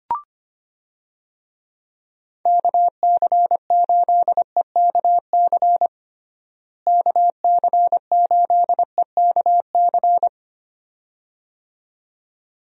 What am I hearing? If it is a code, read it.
Morse code, heard as a keyed tone, about 25 wpm, 700 Hz. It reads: KC8EKC KC8EKC
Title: USA Call Signs - 6 Characters - Repeated 2x - 25wpm